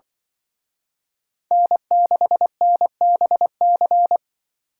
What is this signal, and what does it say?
Morse code, heard as a keyed tone, about 24 wpm, 700 Hz. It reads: N6NBC
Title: Speed Racing - USA Call Signs 20wpm